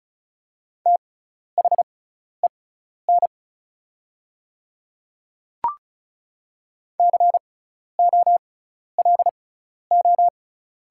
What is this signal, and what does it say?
Morse code, heard as a keyed tone, about 34 wpm, spaced as for 12 wpm, 700 Hz. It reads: THEN COLO
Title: 6x Character Spacing - Top 300 Words - 35wpm